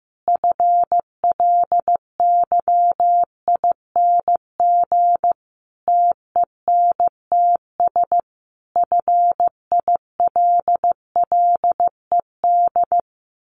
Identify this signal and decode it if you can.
Morse code, heard as a keyed tone, about 14 wpm, 700 Hz. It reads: FLYING TENTS FILLED
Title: A Sound Of Thunder 15wpm